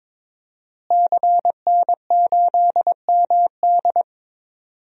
Morse code, heard as a keyed tone, about 22 wpm, 700 Hz. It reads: CN8MD
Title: DXCC-100 Call Signs - 5 Characters - Repeated 2x - 22wpm